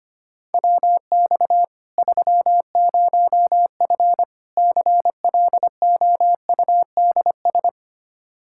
Morse code, heard as a keyed tone, about 25 wpm, 700 Hz. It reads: WX 30F CLOUDS
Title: Practice QSO - Exchange 2 - Info - 25wpm